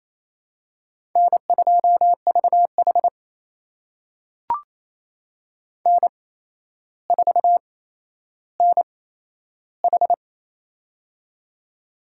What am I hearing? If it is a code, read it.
Morse code, heard as a keyed tone, about 27 wpm, spaced as for 12 wpm, 700 Hz. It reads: N2VH N4NH
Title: USA Call Signs - 4 Characters - 8x Character Spacing - 28wpm